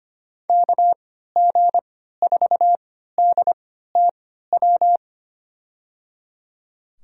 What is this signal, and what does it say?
Morse code, heard as a keyed tone, about 25 wpm, spaced as for 14 wpm, 700 Hz. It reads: KG4DTW
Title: USA Call Signs - 3x Character Spacing - 25wpm